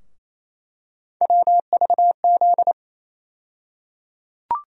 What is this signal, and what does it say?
Morse code, heard as a keyed tone, about 28 wpm, 700 Hz. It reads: WVZ
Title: USA Call Sign Postfixes - 3 Letters 28wpm